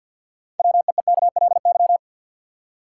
Morse code, heard as a keyed tone, about 53 wpm, 700 Hz. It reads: WEEKLY
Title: Six Letter Words - 50wpm